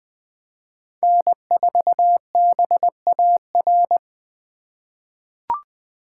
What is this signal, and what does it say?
Morse code, heard as a keyed tone, about 21 wpm, 700 Hz. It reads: N4BAR
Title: USA Call Signs - 5 Characters - 3x Character Spacing - 20wpm